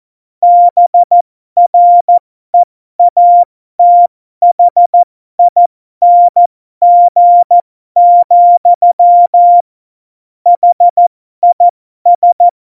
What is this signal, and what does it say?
Morse code, heard as a keyed tone, about 14 wpm, spaced as for 12 wpm, 700 Hz. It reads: BREATHING, HIS
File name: Morsekurs / Witze